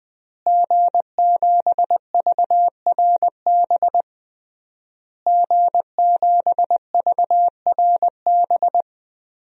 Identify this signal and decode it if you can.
Morse code, heard as a keyed tone, about 20 wpm, 700 Hz. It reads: G7VRB G7VRB
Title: DXCC-100 Call Signs - 5 Characters - Repeated 3x - 20wpm